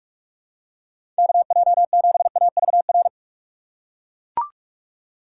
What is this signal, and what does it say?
Morse code, heard as a keyed tone, about 41 wpm, 700 Hz. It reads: KJ7AVR
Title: USA Call Signs - 6x Character Spacing - 45wpm